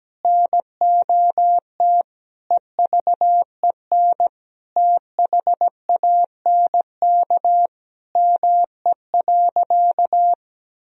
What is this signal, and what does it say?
Morse code, heard as a keyed tone, about 17 wpm, 700 Hz. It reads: NOT EVEN THANK ME.
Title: Sentences from Top 1000 Words - Review Part 1 - 17wpm